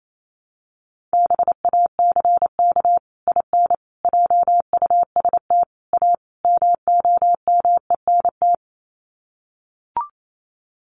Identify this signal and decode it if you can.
Morse code, heard as a keyed tone, about 28 wpm, 700 Hz. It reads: BACK IN JUST A MOMENT
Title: Sets of 5 Words Encore 28wpm